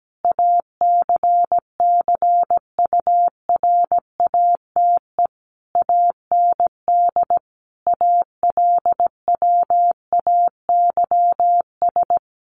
Morse code, heard as a keyed tone, about 17 wpm, 700 Hz. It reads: ACCURATE AND ALWAYS
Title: Sentences from Top 1200 Words 17wpm